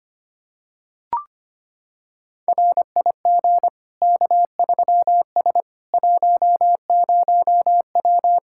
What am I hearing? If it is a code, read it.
Morse code, heard as a keyed tone, about 25 wpm, 700 Hz. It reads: RIG K3S 10W
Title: Practice QSO - Exchange 2 - Info - 25wpm